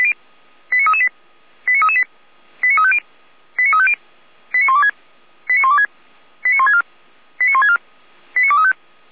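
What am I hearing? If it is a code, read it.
Five-tone selective call (ZVEI), heard as quick runs of five tones, tones of about 80 ms. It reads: 89338 89338 89350 89350 89126 89126 89164 89164 89235
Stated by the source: ZVEI five-tone selective call received by SDR, fire brigade Orlamünde (feuerwehr-orlamuende).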